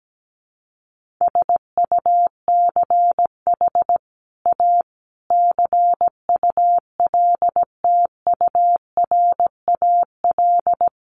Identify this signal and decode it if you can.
Morse code, heard as a keyed tone, about 17 wpm, 700 Hz. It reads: SUCH A CULTURAL
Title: Sentences from Top 1500 Words 17wpm